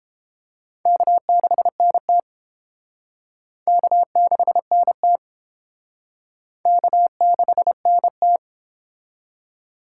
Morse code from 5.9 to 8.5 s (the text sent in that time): K6NT